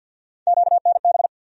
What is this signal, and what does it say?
Morse code, heard as a keyed tone, about 52 wpm, 700 Hz. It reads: QNB
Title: USA Call Sign Postfixes - 3 Letters 50wpm